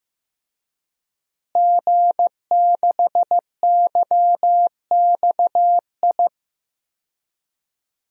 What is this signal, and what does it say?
Morse code, heard as a keyed tone, about 14 wpm, 700 Hz. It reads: G6YXI
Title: DXCC-100 Call Signs - 5 Characters - Repeated 3x - 15wpm